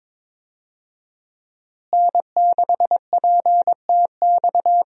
Morse code, heard as a keyed tone, about 22 wpm, 700 Hz. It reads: N6PTX
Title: USA Call Signs - 5 Characters - Repeated 3x - 22wpm